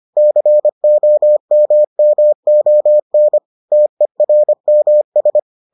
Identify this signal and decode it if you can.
Morse code, heard as a keyed tone, about 25 wpm, 600 Hz. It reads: COMMON TERMS